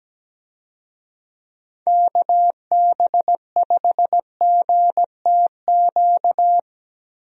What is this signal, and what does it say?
Morse code, heard as a keyed tone, about 17 wpm, 700 Hz. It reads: KB5GTQ
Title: USA Call Signs - 6 Characters - Repeated 2x - 17wpm